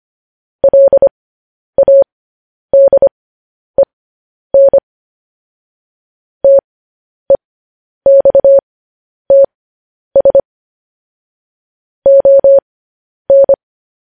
Morse code, heard as a keyed tone, about 25 wpm, spaced as for 10 wpm, 550 Hz. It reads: LADEN TEXTS ON
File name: podcast / News headlines-1x-25x10